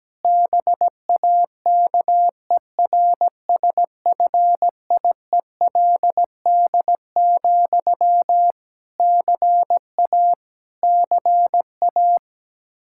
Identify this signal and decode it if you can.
Morse code, heard as a keyed tone, about 17 wpm, 700 Hz. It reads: BAKERSFIELD, CA CA
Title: Practice QSO - Exchange 1 - Beginning - 17wpm